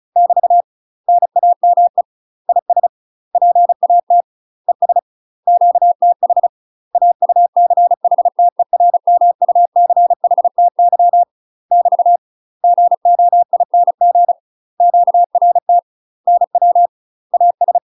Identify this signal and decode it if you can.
Morse code, heard as a keyed tone, about 34 wpm, 700 Hz. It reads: <BT> NAME IS PAT ES QTH AUCHTERMUCHTY <BT> GOING QRT NW AS
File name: UK MASTER QSOs TRAINING / 35WPM / UK MASTER QSOs 0042